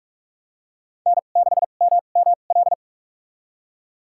Morse code, heard as a keyed tone, about 47 wpm, spaced as for 33 wpm, 690 Hz. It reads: N6MML
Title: USA Call Signs - 5 Characters - 2x Character Spacing - 45wpm